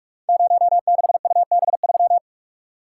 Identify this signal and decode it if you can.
Morse code, heard as a keyed tone, about 44 wpm, 700 Hz. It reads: 06UB3
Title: Code Groups - 5 Letter:Numbers - 45wpm